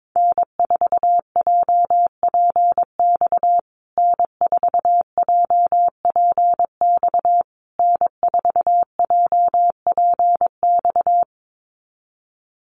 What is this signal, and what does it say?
Morse code, heard as a keyed tone, about 22 wpm, 700 Hz. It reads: N4JPX N4JPX N4JPX
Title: USA Call Signs Repeated 3x 22wpm